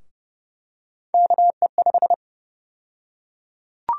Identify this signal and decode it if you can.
Morse code, heard as a keyed tone, about 32 wpm, 700 Hz. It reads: KE5
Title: USA Call Sign Prefixes - Repeated 2x - 30wpm